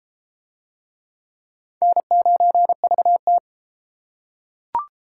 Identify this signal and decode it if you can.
Morse code, heard as a keyed tone, about 33 wpm, 700 Hz. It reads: N9VT